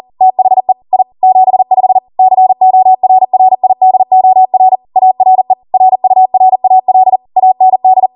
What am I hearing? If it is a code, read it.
Morse code, heard as a keyed tone, about 40 wpm, 750 Hz. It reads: THE I 75 CORRIDOR ARE RURAL AND